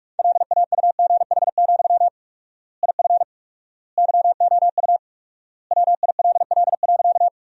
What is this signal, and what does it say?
Morse code, heard as a keyed tone, about 42 wpm, 700 Hz. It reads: LAUGH, IF YOU WILL.
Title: 4x Word-Spacing - Sentences from Top 500 Words - 45wpm